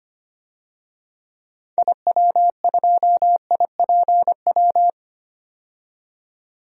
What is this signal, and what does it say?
Morse code, heard as a keyed tone, about 25 wpm, 700 Hz. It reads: IW2IPW
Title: DXCC-100 Call Signs - 6 Characters - 25wpm